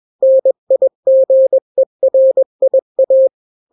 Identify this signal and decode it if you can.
Morse code, heard as a keyed tone, about 21 wpm, 530 Hz. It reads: NIGERIA
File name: daily / 20wpm